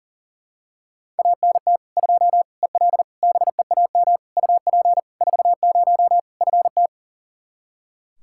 Morse code, heard as a keyed tone, about 37 wpm, 700 Hz. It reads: ANT 2 EL BEAM UP 40 FT
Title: Practice QSO - Exchange 2 - Info - 40wpm